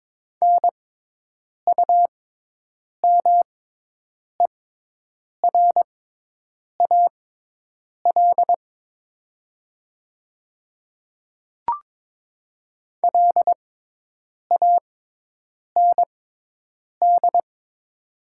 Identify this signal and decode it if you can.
Morse code, heard as a keyed tone, about 22 wpm, spaced as for 8 wpm, 700 Hz. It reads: NUMERAL LAND
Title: 6x Character Spacing - Top 500 Words - 22wpm